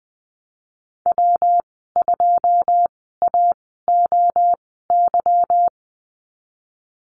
Morse code, heard as a keyed tone, about 20 wpm, spaced as for 15 wpm, 700 Hz. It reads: W2AOY
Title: 161 Lesson 26g - Letter g - Call Signs 20wpm